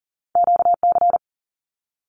Morse code, heard as a keyed tone, about 41 wpm, 700 Hz. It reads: QC
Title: Canadian Provinces and Territories 40wpm